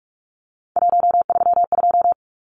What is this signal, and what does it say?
Morse code, heard as a keyed tone, about 44 wpm, 700 Hz. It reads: 132
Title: Numbers - 3-Digit 45wpm